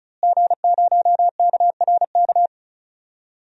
Morse code, heard as a keyed tone, about 34 wpm, 700 Hz. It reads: G0KRK
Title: DXCC-100 Call Signs - 5 Characters - Repeated 3x - 35wpm